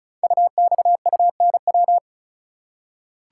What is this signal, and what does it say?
Morse code, heard as a keyed tone, about 36 wpm, 690 Hz. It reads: UXUNW
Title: Code Groups - 5 Letters - 35wpm